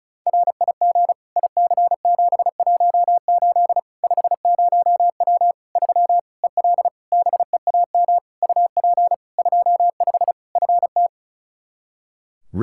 Morse code, heard as a keyed tone, about 34 wpm, 700 Hz. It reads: RIG IC718 50W 3 EL BEAM UP 25 FT
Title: Practice QSO - Exchange 2 - Info - 35wpm